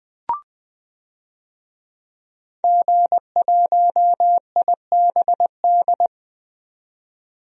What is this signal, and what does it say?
Morse code, heard as a keyed tone, about 20 wpm, 700 Hz. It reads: G1IBD
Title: DXCC-100 Call Signs - 5 Characters - 20wpm